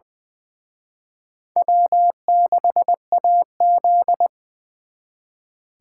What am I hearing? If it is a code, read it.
Morse code, heard as a keyed tone, about 23 wpm, 700 Hz. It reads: W6AZ
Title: Speed Racing - USA 1x2 Call Signs 20wpm